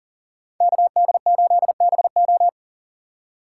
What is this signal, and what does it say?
Morse code, heard as a keyed tone, about 40 wpm, 700 Hz. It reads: KD8BO
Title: USA Call Signs - 4x Character Spacing - 40wpm